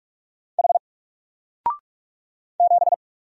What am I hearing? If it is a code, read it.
Morse code, heard as a keyed tone, about 45 wpm, 700 Hz. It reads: H 7
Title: Single Letter-Number 45wpm